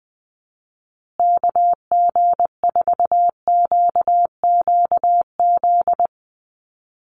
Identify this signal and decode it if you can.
Morse code, heard as a keyed tone, about 20 wpm, 700 Hz. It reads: KG4QQZ
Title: USA Call Signs Repeated 2x 20wpm